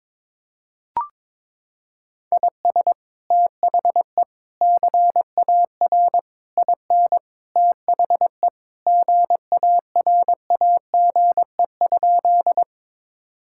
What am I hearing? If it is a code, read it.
Morse code, heard as a keyed tone, about 22 wpm, 700 Hz. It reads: IS THE CAR IN THE GARAGE?
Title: Sentences from Top 1600 Words 22wpm